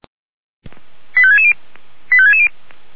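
Five-tone selective call (ZVEI), heard as quick runs of five tones, tones of about 70 ms. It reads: 85660 85660
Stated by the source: ZVEI five-tone selective call received by SDR.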